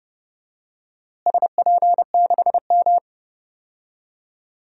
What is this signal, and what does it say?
Morse code, heard as a keyed tone, about 30 wpm, 700 Hz. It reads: SP6M
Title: DXCC-100 Call Signs - 4 Characters - Repeated 2x - 30wpm